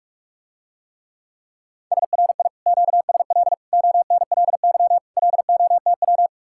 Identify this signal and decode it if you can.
Morse code, heard as a keyed tone, about 49 wpm, 700 Hz. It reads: SRI QSL ONLY LOTW